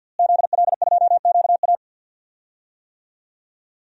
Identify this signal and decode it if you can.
Morse code, heard as a keyed tone, about 52 wpm, 700 Hz. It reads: ZL2QA
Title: DXCC-100 Call Signs - 5 Characters - Repeated 2x - 50wpm